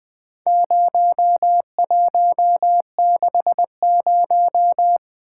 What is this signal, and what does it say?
Morse code, heard as a keyed tone, about 21 wpm, 700 Hz. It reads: 0160
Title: Four Letter-Number Combinations - 20wpm